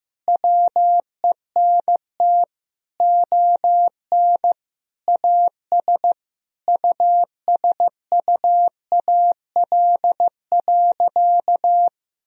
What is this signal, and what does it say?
Morse code, heard as a keyed tone, about 15 wpm, 700 Hz. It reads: WENT ON AS USUAL.
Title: Sentences from Top 1000 Words - Review Part 2 - 15wpm